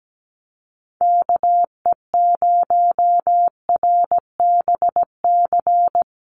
Morse code, heard as a keyed tone, about 17 wpm, 700 Hz. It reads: KE0RBC
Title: USA Call Signs Repeated 2x 17wpm